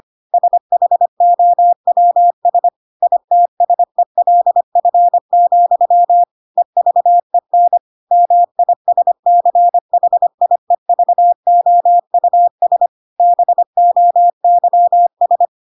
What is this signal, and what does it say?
Morse code, heard as a keyed tone, about 25 wpm, 700 Hz. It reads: SHOWS ITSELF, EVEN MISCHIEVOUS BOYS
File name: Pinocchio / Pinocchio ch8 25 25 ik1hge